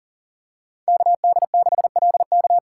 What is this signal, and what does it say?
Morse code, heard as a keyed tone, about 40 wpm, 700 Hz. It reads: KD6LK
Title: USA Call Signs - 5 Characters - 2x Character Spacing - 40wpm